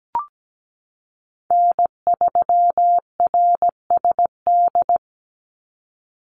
Morse code, heard as a keyed tone, about 17 wpm, 700 Hz. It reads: N3RSD
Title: Call Signs Repeated 3x - 17wpm